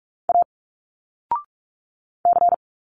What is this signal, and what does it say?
Morse code, heard as a keyed tone, about 46 wpm, 700 Hz. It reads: A C
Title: Single Letters 45wpm